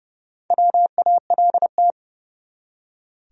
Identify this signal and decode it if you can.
Morse code, heard as a keyed tone, about 30 wpm, 700 Hz. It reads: WALT
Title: USA Common Names - Max 4 Letters - 30wpm